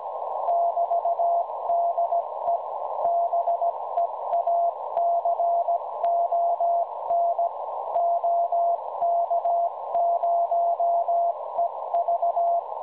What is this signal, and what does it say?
Morse code, heard as a keyed tone, about 16 wpm, 690 Hz, with noise. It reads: <BT>DE BEACON OK0EV
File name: WebSDR records / OK0EV websdr recording 1854.0kHz